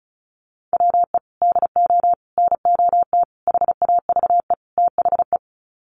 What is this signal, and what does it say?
Morse code, heard as a keyed tone, about 34 wpm, 700 Hz. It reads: WE DO NOT HAVE THE
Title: Sets of 5 Words 35wpm